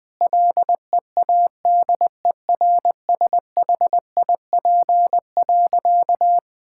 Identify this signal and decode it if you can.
Morse code, heard as a keyed tone, about 20 wpm, 700 Hz. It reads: LEADERSHIP.
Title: Sentences from Top 1600 Words 20wpm